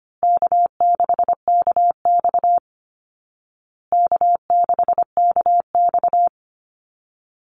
Morse code, heard as a keyed tone, about 25 wpm, 700 Hz. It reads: K6KX K6KX
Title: Easy Call Signs Repeated 3x - 25wpm